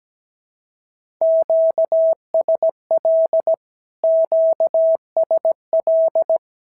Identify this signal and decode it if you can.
Morse code, heard as a keyed tone, about 17 wpm, 650 Hz. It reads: QSL QSL
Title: SOTA QSO Practice 17wpm